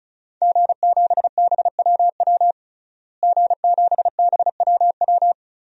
Morse code, heard as a keyed tone, about 36 wpm, 700 Hz. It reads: G7BWW G7BWW
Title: DXCC-100 Call Signs - 5 Characters - Repeated 2x - 35wpm